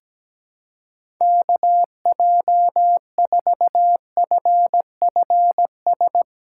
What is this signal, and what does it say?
Morse code, heard as a keyed tone, about 17 wpm, 700 Hz. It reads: KJ4FFS
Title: USA Callsign DE Callsign - 6 Characters - 17wpm